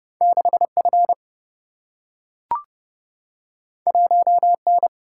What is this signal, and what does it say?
Morse code, heard as a keyed tone, about 30 wpm, 700 Hz. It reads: 6F 1N
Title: Single-Letter Single-Number 30wpm